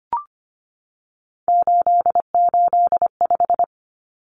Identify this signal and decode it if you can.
Morse code, heard as a keyed tone, about 25 wpm, 700 Hz. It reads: 885
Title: Numbers - 3-Digit 25wpm